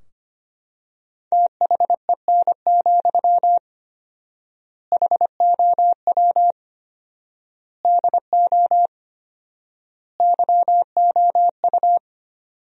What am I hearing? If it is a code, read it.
Morse code, heard as a keyed tone, about 25 wpm, 700 Hz. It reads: THEN, HOW DO YOU
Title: 4x Word-Spacing - Sentences from Top 100 Words - 25wpm